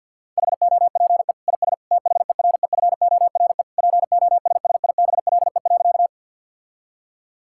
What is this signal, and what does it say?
Morse code, heard as a keyed tone, about 52 wpm, 700 Hz. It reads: HOPE IS THEREFORE POSSIBLE.
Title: Sentences from Top 1000 Words - Review Part 5 - 50wpm